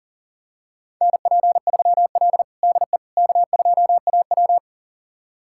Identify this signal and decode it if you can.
Morse code, heard as a keyed tone, about 40 wpm, 700 Hz. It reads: NP3L DE K2AW
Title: USA Callsign DE Callsign - 4 Characters - 40wpm